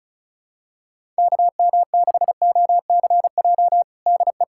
Morse code, heard as a keyed tone, about 34 wpm, 700 Hz. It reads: KM6OCJ DE
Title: USA Callsign DE Callsign - 6 Characters - 35wpm